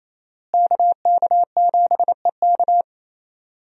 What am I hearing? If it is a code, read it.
Morse code, heard as a keyed tone, about 28 wpm, 700 Hz. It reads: KK7EK
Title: USA Call Signs - 5 Characters - Repeated 2x - 28wpm